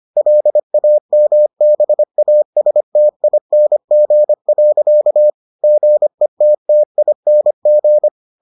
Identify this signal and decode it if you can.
Morse code, heard as a keyed tone, about 25 wpm, 600 Hz. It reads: LAMBASTING. GETTING